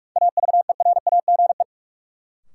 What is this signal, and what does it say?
Morse code, heard as a keyed tone, about 46 wpm, 700 Hz. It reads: AVERAGE
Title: Sentences from Top 1400 Words - New Word Drill Alt - 45wpm